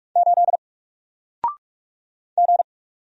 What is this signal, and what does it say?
Morse code, heard as a keyed tone, about 44 wpm, 700 Hz. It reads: 8 G
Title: Single Character (Letter:Number:Punctuation:Prosign) Alt 45wpm